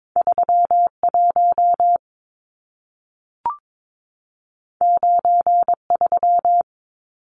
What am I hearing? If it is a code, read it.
Morse code, heard as a keyed tone, about 22 wpm, 700 Hz. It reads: 31 93
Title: Numbers - 2-Digit 22wpm